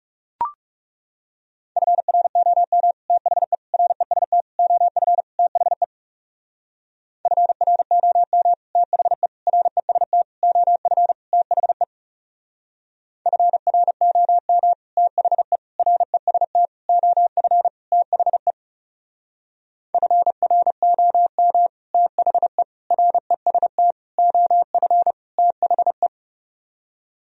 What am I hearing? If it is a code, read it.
Morse code, heard as a keyed tone, about 35 wpm, 700 Hz. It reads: FROM THE REST OF THE FROM THE REST OF THE FROM THE REST OF THE FROM THE REST OF THE
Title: Speed-Racing - Sets of 5 Words 30wpm